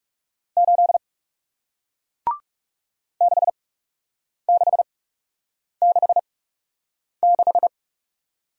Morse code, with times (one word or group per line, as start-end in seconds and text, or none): 0.6-1.0 8
3.2-7.7 6666